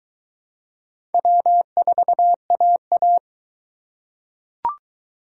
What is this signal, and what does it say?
Morse code, heard as a keyed tone, about 20 wpm, 700 Hz. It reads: W4AA